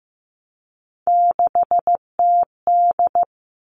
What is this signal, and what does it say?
Morse code, heard as a keyed tone, about 15 wpm, 700 Hz. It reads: 6TD
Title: All Cut Numbers - 3-Digit 15wpm